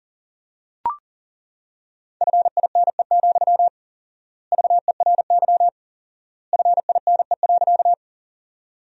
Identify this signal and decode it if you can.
Morse code, heard as a keyed tone, about 40 wpm, 700 Hz. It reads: FINE, VERY FINE.
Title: 4x Word-Spacing - Sentences from Top 500 Words - 40wpm